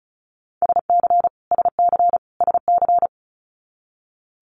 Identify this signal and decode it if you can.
Morse code, heard as a keyed tone, about 36 wpm, 700 Hz. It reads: SC SC SC
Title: Multi-letter Phonemes Repeated 3x 35wpm